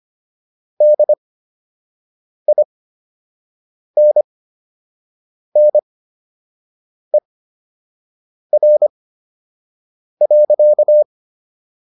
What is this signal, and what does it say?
Morse code, heard as a keyed tone, about 25 wpm, spaced as for 6 wpm, 600 Hz. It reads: DINNER.